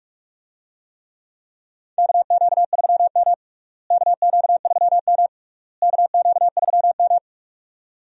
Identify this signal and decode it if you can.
Morse code, heard as a keyed tone, about 46 wpm, 700 Hz. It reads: KQ3M KQ3M KQ3M
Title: USA Call Signs - 4 Characters - Repeated 3x - 45wpm